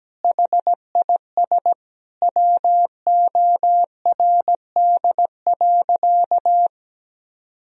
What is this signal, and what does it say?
Morse code, heard as a keyed tone, about 17 wpm, 700 Hz. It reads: HIS WORD.
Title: Sentences from Top 1000 Words - Review Part 1 - 17wpm